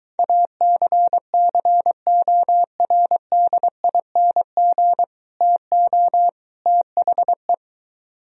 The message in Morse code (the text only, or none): ACCORDING TO THE